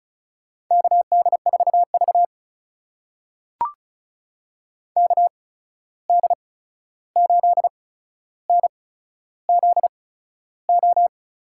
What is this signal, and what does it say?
Morse code, heard as a keyed tone, about 36 wpm, spaced as for 13 wpm, 700 Hz. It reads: KD4V KD8NZO
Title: USA Call Signs - 8x Character Spacing - 35wpm